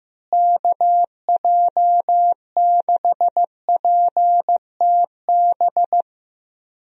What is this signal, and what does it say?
Morse code, heard as a keyed tone, about 15 wpm, 700 Hz. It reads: KJ6PTB
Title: USA Call Signs - 6 Characters - Repeated 3x - 15wpm